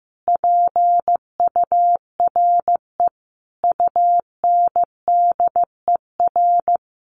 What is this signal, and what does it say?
Morse code, heard as a keyed tone, about 15 wpm, 700 Hz. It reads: PURE UNDER
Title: Sentences from Top 1500 Words 15wpm